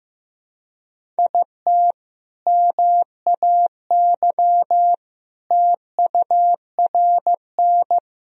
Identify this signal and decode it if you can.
Morse code, heard as a keyed tone, about 14 wpm, 700 Hz. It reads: IT MAY TURN
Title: Sentences from Top 400 Words and Abbreviations 15wpm